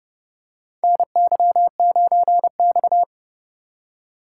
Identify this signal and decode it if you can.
Morse code, heard as a keyed tone, about 31 wpm, 700 Hz. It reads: NY9X
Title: K1USN SST QSO Practice - 30wpm